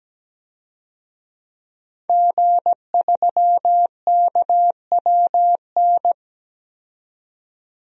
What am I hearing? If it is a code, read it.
Morse code, heard as a keyed tone, about 17 wpm, 700 Hz. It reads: G3KWN
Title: DXCC-100 Call Signs - 5 Characters - 17wpm